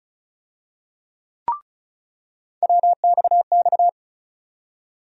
Morse code, heard as a keyed tone, about 36 wpm, 700 Hz. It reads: WXX
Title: USA Call Sign Postfixes - 3 Letters 35wpm